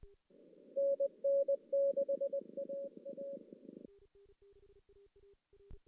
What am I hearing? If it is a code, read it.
Morse code, heard as a keyed tone, about 20 wpm, 560 Hz. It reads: NN6AA
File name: 2016 JAN CW NAQP / 2016 Jan naqp k5ot - 0-5